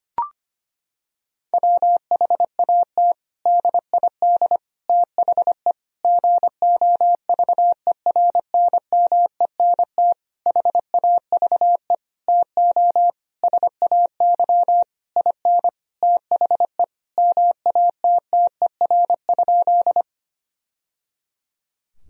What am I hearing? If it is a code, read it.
Morse code, heard as a keyed tone, about 25 wpm, 700 Hz. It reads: WHAT DID THE GOVERNMENT HAVE TO SAY IN THE MATTER?
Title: Sentences from Top 1000 Words - Review Part 3 - 25wpm